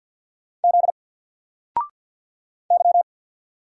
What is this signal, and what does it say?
Morse code, heard as a keyed tone, about 48 wpm, 700 Hz. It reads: Z Y
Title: Single Character (Letter:Number:Punctuation:Prosign) Alt 50wpm